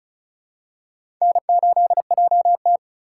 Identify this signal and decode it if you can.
Morse code, heard as a keyed tone, about 34 wpm, 700 Hz. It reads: N8JT
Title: USA Call Signs - 4 Characters - 4x Character Spacing - 35wpm